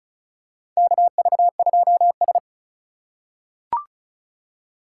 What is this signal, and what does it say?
Morse code, heard as a keyed tone, about 36 wpm, 700 Hz. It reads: KV2S
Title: USA Call Signs - 4 Characters - 4x Character Spacing - 35wpm